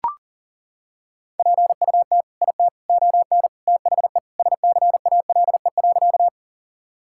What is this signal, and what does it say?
Morse code, heard as a keyed tone, about 40 wpm, 700 Hz. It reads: PUT IT ON THE SCALE.
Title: Sentences from Top 1000 Words - Review Part 5 - 40wpm